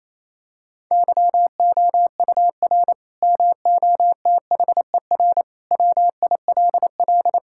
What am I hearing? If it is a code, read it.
Morse code, heard as a keyed tone, about 28 wpm, 700 Hz. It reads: YOUR MOTHER WILL
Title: Sentences from Top 1600 Words 28wpm